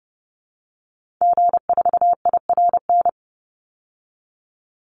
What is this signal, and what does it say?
Morse code, heard as a keyed tone, about 30 wpm, 700 Hz. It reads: G4IRN
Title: Call Signs Repeated 3x - 30wpm